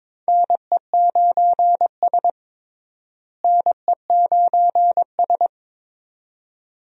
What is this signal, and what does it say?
Morse code, heard as a keyed tone, about 22 wpm, 700 Hz. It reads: NE9S NE9S
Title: USA Call Signs - 4 Characters - Repeated 2x - 22wpm